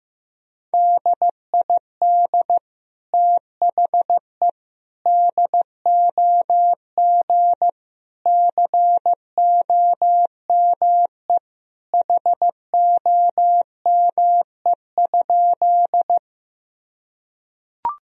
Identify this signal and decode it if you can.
Morse code, heard as a keyed tone, about 15 wpm, 700 Hz. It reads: DID THE DOG COME HOME?
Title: Sentences from Top 1000 Words - Review Part 2 - 15wpm